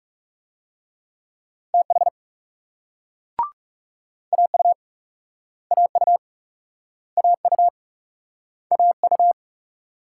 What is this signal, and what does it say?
Morse code, heard as a keyed tone, about 38 wpm, 700 Hz. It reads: TH AU AU AU AU